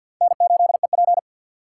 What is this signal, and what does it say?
Morse code, heard as a keyed tone, about 47 wpm, 700 Hz. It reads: N8EP
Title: USA Call Signs - 6x Character Spacing - 50wpm